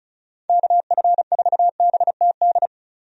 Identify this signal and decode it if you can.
Morse code, heard as a keyed tone, about 34 wpm, 700 Hz. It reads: KF4BTD
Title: USA Call Signs - 6 Characters - 35wpm